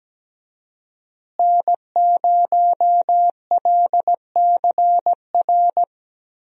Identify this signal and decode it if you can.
Morse code, heard as a keyed tone, about 17 wpm, 700 Hz. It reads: N0LCR
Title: USA Call Signs - 5 Characters - Repeated 2x - 17wpm